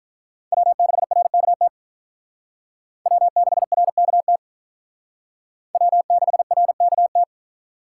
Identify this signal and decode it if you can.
Morse code, heard as a keyed tone, about 46 wpm, 700 Hz. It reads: W6RKT W6RKT W6RKT